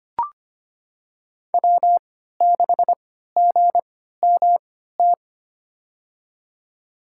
Morse code, heard as a keyed tone, about 25 wpm, spaced as for 14 wpm, 700 Hz. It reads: W6GMT
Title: USA Call Signs - 5 Characters - 3x Character Spacing - 25wpm